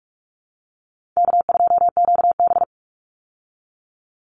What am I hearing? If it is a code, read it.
Morse code, heard as a keyed tone, about 44 wpm, 700 Hz. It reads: K2QB
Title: Call Signs Repeated 3x - 45wpm